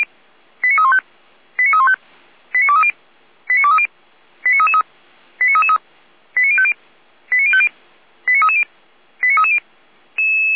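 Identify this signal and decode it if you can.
Five-tone selective call (ZVEI), heard as quick runs of five tones, tones of about 70 ms. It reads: 89315 89315 89230 89230 89333 89333 89050 89050 89330 89330
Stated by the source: ZVEI five-tone selective call received on SDR, code 89304.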